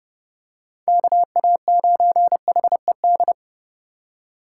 Morse code, heard as a keyed tone, about 28 wpm, 700 Hz. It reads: KA9HED